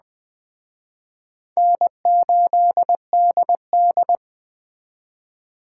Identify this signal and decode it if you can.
Morse code, heard as a keyed tone, about 21 wpm, 700 Hz. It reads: N8DD